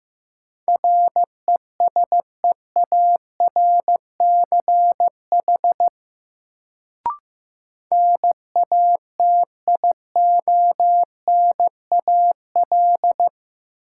Morse code, heard as a keyed tone, about 15 wpm, 700 Hz. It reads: RESEARCH NATIONAL
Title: Sentences from Top 1000 Words - Review Part 4 - New Word Drill - 15wpm